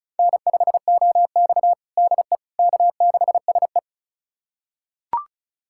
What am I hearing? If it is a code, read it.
Morse code, heard as a keyed tone, about 34 wpm, 700 Hz. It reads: N5OX DE K6SE
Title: USA Callsign DE Callsign - 4 Characters - 35wpm